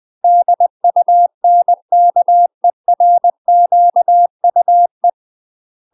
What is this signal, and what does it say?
Morse code, heard as a keyed tone, about 20 wpm, 700 Hz. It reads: DUNKERQUE
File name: UK MASTER QSOs TRAINING / 20WPM / UK MASTER QSOs 0041